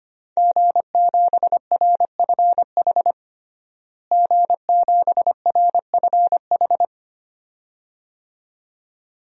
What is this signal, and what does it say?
Morse code, heard as a keyed tone, about 25 wpm, 700 Hz. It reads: G7RFH G7RFH
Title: DXCC-100 Call Signs - 5 Characters - Repeated 2x - 25wpm